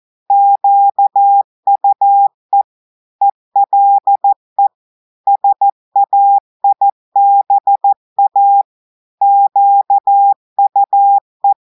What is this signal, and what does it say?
Morse code, heard as a keyed tone, about 14 wpm, 800 Hz. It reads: QUE ELE SAIBA QUE
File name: audios / portugais / portugais texte en clair 2 vitesse 14 mots minutes